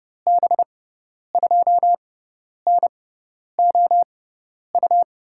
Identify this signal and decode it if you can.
Morse code, heard as a keyed tone, about 32 wpm, spaced as for 10 wpm, 700 Hz. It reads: B2NOU
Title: USA Call Signs - 6 Characters - 6x Character Spacing - 30wpm